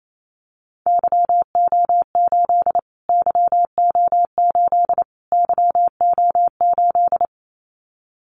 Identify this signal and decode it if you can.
Morse code, heard as a keyed tone, about 28 wpm, 700 Hz. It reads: YO8 YO8 YO8
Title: Callsign Prefixes for Top 25 DXCC - Repeated 3x - 28wpm